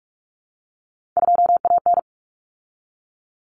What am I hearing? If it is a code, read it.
Morse code, heard as a keyed tone, about 44 wpm, 700 Hz. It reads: 2AN
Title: Common Cut Numbers - 3-Digit 45wpm